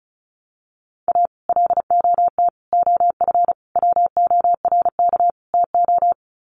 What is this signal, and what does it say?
Morse code, heard as a keyed tone, about 34 wpm, 700 Hz. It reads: A LOT OF WORK TO
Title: Sets of 5 Words Encore2 35wpm